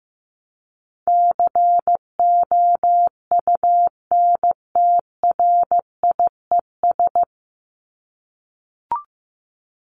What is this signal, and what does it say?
Morse code, heard as a keyed tone, about 15 wpm, 700 Hz. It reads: COUNTRIES
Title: ES:S Suffix - Top 100 Words with Base Words 15wpm